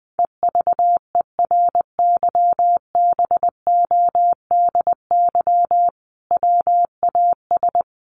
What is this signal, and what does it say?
Morse code, heard as a keyed tone, about 20 wpm, 700 Hz. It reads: EVERYBODY WAS
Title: Sentences from Top 1400 Words 20wpm